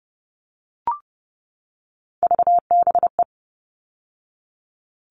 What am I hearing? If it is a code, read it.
Morse code, heard as a keyed tone, about 30 wpm, 700 Hz. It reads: VBE
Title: All Cut Numbers - 3-Digit 30wpm